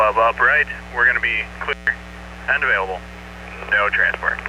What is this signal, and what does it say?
Emergency Alert System (SAME) tones, heard as harsh buzzing bursts, with speech. It is heard from 0.9 to 4.2 s.